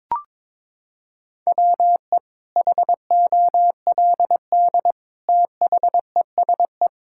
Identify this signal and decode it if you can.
Morse code, heard as a keyed tone, about 21 wpm, 700 Hz. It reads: WE HOLD THESE
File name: Sentences with 9-Word Phrases - 22wpm